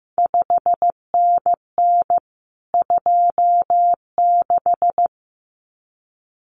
Morse code, heard as a keyed tone, about 15 wpm, 700 Hz. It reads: 5NN 26
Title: CQ WW DX Contest Exchange Encore - 15wpm